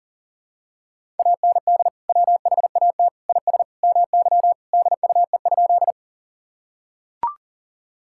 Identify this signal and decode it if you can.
Morse code, heard as a keyed tone, about 37 wpm, 700 Hz. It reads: AND WHAT IS MY DUE?
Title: Sentences from Top 1000 Words - Review Part 6 - 40wpm